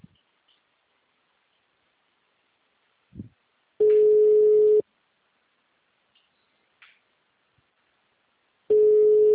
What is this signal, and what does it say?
Ringback tone (Europe, 425 Hz).